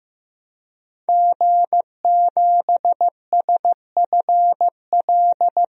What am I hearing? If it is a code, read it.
Morse code, heard as a keyed tone, about 16 wpm, 700 Hz. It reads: G7SFL
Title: DXCC-100 Call Signs - 5 Characters - 15wpm